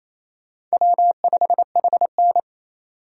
Morse code, heard as a keyed tone, about 29 wpm, 700 Hz. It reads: W5HN
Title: Speed Racing - USA 1x2 Call Signs 28wpm